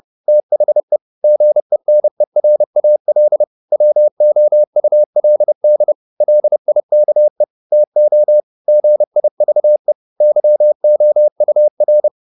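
Morse code, heard as a keyed tone, about 30 wpm, 600 Hz. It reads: THE GENERAL WOULD LIKE TO GIVE YOUR
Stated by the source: Morse code at 30 wpm, the novel Never by Ken Follett in English, from CW med Gnister.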